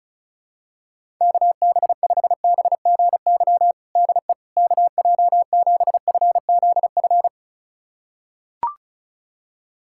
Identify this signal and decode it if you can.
Morse code, heard as a keyed tone, about 36 wpm, 700 Hz. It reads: KB5BGY DE KJ7FZF
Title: USA Callsign DE Callsign 35wpm